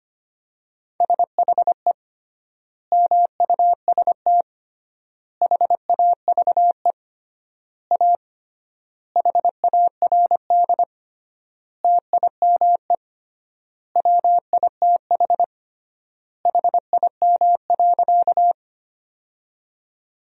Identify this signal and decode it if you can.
Morse code, heard as a keyed tone, about 25 wpm, 700 Hz. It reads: SHE MUST HAVE A HARD TIME WITH HIM.
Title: 3x Word-Spacing - Sentences from Top 300 Words - 25wpm